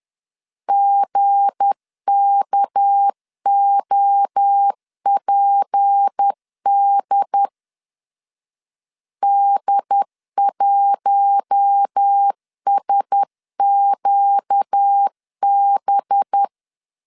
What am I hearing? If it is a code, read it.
Morse code, heard as a keyed tone, about 10 wpm, 800 Hz. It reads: GKOPD D1SQB
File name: LEZIONI / 7 GF9Z / 7 GF 10wpm 2514